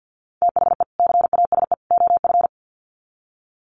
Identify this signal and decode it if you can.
Morse code, heard as a keyed tone, about 50 wpm, 700 Hz. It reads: THE CASE OF
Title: Sets of 3 Words 50wpm